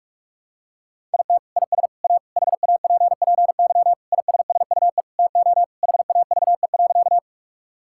Morse code, heard as a keyed tone, about 46 wpm, 700 Hz. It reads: IT IS A HAPPY ISSUE TO HAVE.
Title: Sentences from Top 1000 Words - Review Part 6 - 45wpm